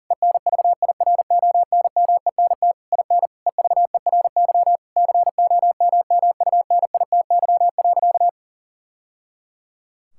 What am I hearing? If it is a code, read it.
Morse code, heard as a keyed tone, about 40 wpm, 700 Hz. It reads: ENVIRONMENT IN EVERY COMMUNITY.
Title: Sentences from Top 1000 Words - Review Part 5 - 40wpm